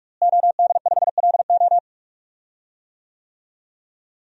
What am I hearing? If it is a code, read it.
Morse code, heard as a keyed tone, about 46 wpm, 700 Hz. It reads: OD5LO
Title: DXCC-100 Call Signs - 5 Characters - Repeated 3x - 45wpm